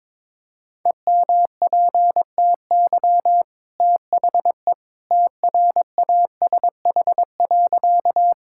Morse code, heard as a keyed tone, about 21 wpm, 690 Hz. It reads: EMPTY THE TRASH.
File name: practice / 22wpm / Easy Sentences with Hints - 22wpm